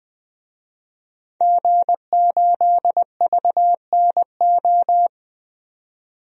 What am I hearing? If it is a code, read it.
Morse code, heard as a keyed tone, about 20 wpm, 700 Hz. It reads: G8VNO
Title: DXCC-100 Call Signs - 5 Characters - 20wpm